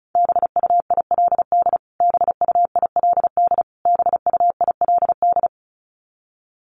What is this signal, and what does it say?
Morse code, heard as a keyed tone, about 34 wpm, 700 Hz. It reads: BUILD BUILD BUILD
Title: Top 500 Words Repeated 35wpm